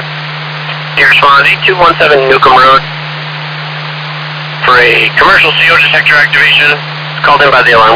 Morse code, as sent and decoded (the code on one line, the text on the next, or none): none